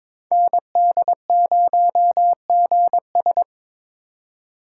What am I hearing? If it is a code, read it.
Morse code, heard as a keyed tone, about 26 wpm, 700 Hz. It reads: ND0GS
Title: Speed Racing - USA Call Signs 22wpm